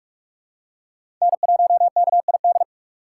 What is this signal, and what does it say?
Morse code, heard as a keyed tone, about 43 wpm, 700 Hz. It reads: N1KID
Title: USA Call Signs - 6x Character Spacing - 45wpm